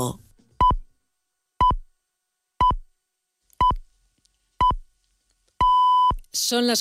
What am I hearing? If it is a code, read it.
Radio time signal (time pips), with speech. It is heard from 0.6 to 6.1 s.